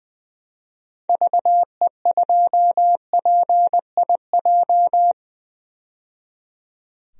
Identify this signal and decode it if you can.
Morse code, heard as a keyed tone, about 20 wpm, 700 Hz. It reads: VE2PIJ
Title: DXCC-100 Call Signs - 6 Characters - 20wpm